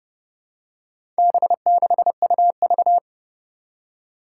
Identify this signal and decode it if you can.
Morse code, heard as a keyed tone, about 31 wpm, 700 Hz. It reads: B6UV
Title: Four Challenging Letter-Number Combinations 30wpm